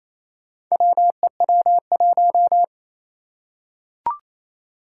Morse code, heard as a keyed tone, about 27 wpm, 700 Hz. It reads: WEW1
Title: Four Challenging Letter-Number Combinations 28wpm